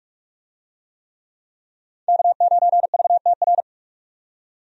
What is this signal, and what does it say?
Morse code, heard as a keyed tone, about 46 wpm, 700 Hz. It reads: K9VTR
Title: USA Call Signs - 5 Characters - 45wpm